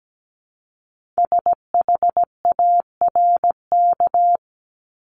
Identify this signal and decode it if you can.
Morse code, heard as a keyed tone, about 17 wpm, 700 Hz. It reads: SHARK
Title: Multi-letter Phonemes 17wpm